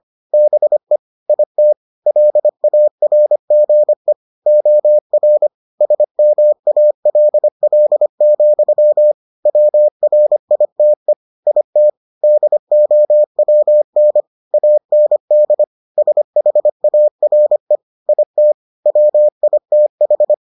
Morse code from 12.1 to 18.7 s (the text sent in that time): DOWN AND SHARE IT